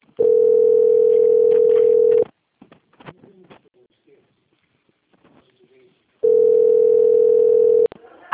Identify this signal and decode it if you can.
Ringback tone (North America), 440 + 480 Hz, with speech.
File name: out- - - - - .37018